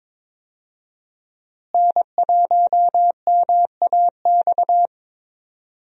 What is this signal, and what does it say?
Morse code, heard as a keyed tone, about 22 wpm, 700 Hz. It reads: N1MAX
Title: USA Call Signs - 5 Characters - Repeated 3x - 22wpm